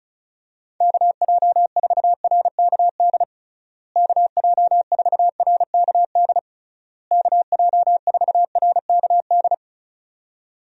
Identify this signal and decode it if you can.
Morse code, heard as a keyed tone, about 36 wpm, 700 Hz. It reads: KJ4RKD KJ4RKD KJ4RKD
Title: USA Call Signs - 6 Characters - Repeated 3x - 35wpm